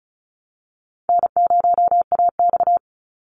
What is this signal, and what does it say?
Morse code, heard as a keyed tone, about 34 wpm, 700 Hz. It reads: N0AX
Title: Call Signs Repeated 2x - 35wpm